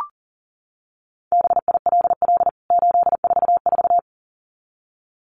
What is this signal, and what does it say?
Morse code, heard as a keyed tone, about 40 wpm, 700 Hz. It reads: BILL 844
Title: CWT Exchange 40wpm